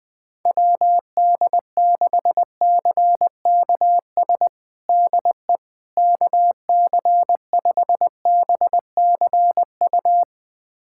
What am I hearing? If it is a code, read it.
Morse code, heard as a keyed tone, about 21 wpm, 700 Hz. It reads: WD6CKS DE KC5BCU
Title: USA Callsign DE Callsign - 6 Characters - 20wpm